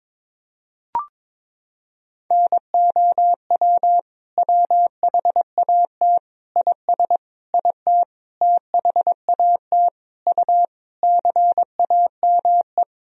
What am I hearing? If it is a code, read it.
Morse code, heard as a keyed tone, about 22 wpm, 700 Hz. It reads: NOW WHAT IS IT THAT U CAME
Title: Sentences from Top 200 Words and Abbreviations 22wpm